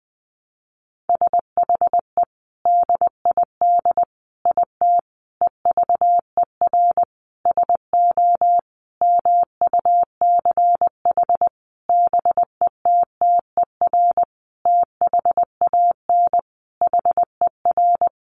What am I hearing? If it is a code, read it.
Morse code, heard as a keyed tone, about 20 wpm, 700 Hz. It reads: SHE DID IT EVER SO MUCH BETTER THAN HER
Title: Sentences from Top 400 Words 20wpm